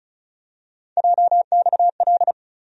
Morse code, heard as a keyed tone, about 36 wpm, 700 Hz. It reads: JXL
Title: USA Call Sign Postfixes - 3 Letters 35wpm